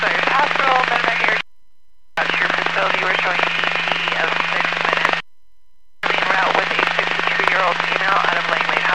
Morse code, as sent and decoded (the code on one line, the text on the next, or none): none